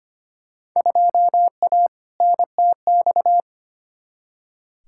Morse code, heard as a keyed tone, about 25 wpm, 700 Hz. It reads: 2A NTX
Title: Field Day Exchanges - 25wpm